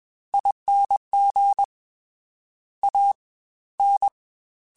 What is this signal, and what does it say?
Morse code, heard as a keyed tone, about 21 wpm, 800 Hz. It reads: ING A N
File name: Special INGs-mp3s / Special INGs-mp3s / Fast ing @20